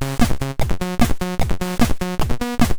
There is no code.